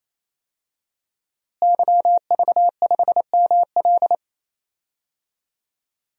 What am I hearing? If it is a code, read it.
Morse code, heard as a keyed tone, about 28 wpm, 700 Hz. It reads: YV5ML
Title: DXCC-100 Call Signs - 5 Characters - Repeated 3x - 28wpm